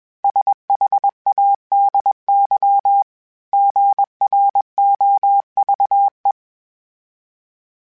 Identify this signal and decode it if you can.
Morse code, heard as a keyed tone, about 21 wpm, 800 Hz. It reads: SHADY GROVE
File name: fromSMs / FromSMs mp3s / Cities@20